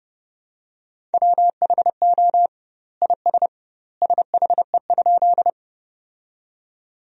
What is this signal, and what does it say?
Morse code, heard as a keyed tone, about 32 wpm, 700 Hz. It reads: WHO IS SHE?
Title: 2x Word-Spacing - Sentences from Top 100 Words - 30wpm